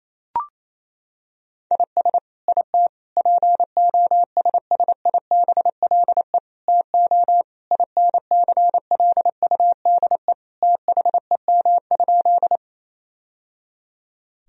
Morse code, heard as a keyed tone, about 28 wpm, 700 Hz. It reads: IS IT POSSIBLE TO INCLUDE THEM?
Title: Sentences from Top 1000 Words - Review Part 3 - 28wpm